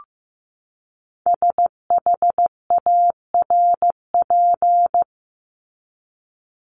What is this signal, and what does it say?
Morse code, heard as a keyed tone, about 15 wpm, 700 Hz. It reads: SHARP
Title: Top 1501-1750 Words 15wpm